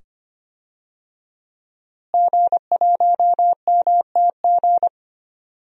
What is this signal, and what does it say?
Morse code, heard as a keyed tone, about 25 wpm, 700 Hz. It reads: G1MTG
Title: DXCC-100 Call Signs - 5 Characters - Repeated 2x - 25wpm